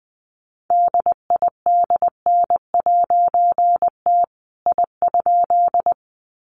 Morse code, heard as a keyed tone, about 20 wpm, 700 Hz. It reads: DIDN'T I?
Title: A Sound Of Thunder 20wpm